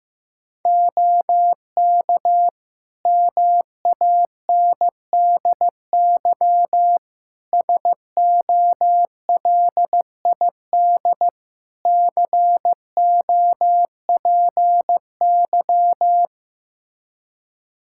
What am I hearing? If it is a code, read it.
Morse code, heard as a keyed tone, about 16 wpm, 700 Hz. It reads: OK MANDY SOLID COPY
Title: Practice QSO - Exchange 3 - More Info - 15wpm